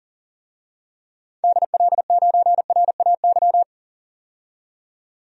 Morse code, heard as a keyed tone, about 40 wpm, 700 Hz. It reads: DL9RAY
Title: DXCC-100 Call Signs - 6 Characters - 40wpm